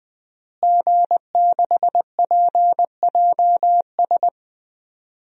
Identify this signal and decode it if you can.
Morse code, heard as a keyed tone, about 20 wpm, 700 Hz. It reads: G6PJS
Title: DXCC-100 Call Signs - 5 Characters - Repeated 2x - 20wpm